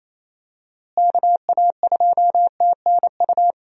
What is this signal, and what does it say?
Morse code, heard as a keyed tone, about 28 wpm, 700 Hz. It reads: KA2TNU
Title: USA Call Signs - 6 Characters - Repeated 2x - 28wpm